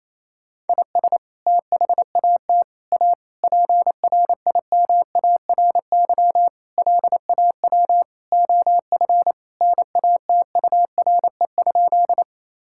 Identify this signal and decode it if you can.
Morse code, heard as a keyed tone, about 27 wpm, 700 Hz. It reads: IS THAT A PRIMARY LAW OF NATURE?
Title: Sentences from Top 1000 Words - Review Part 5 - 28wpm